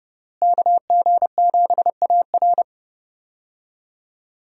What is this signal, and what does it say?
Morse code, heard as a keyed tone, about 30 wpm, 700 Hz. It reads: KG7AR
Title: USA Call Signs - 5 Characters - Repeated 3x - 30wpm